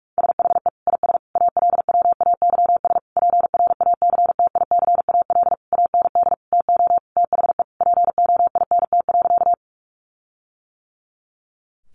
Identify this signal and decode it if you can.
Morse code, heard as a keyed tone, about 44 wpm, 700 Hz. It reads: SHE IS ALWAYS PRACTICAL AND TO THE POINT.
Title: Sentences from Top 1400 Words 45wpm